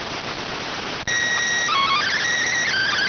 SSTV (slow-scan television), PD 290.